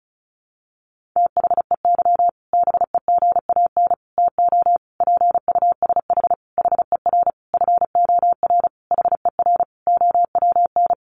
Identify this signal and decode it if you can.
Morse code, heard as a keyed tone, about 34 wpm, 700 Hz. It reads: THEY BEGAN TO PUSH HER FOR HER OWN
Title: Sentences from Top 1300 Words 35wpm